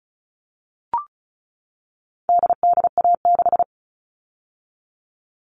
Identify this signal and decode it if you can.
Morse code, heard as a keyed tone, about 34 wpm, 700 Hz. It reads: DDA6
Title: All Cut Numbers - 4-Digit 35wpm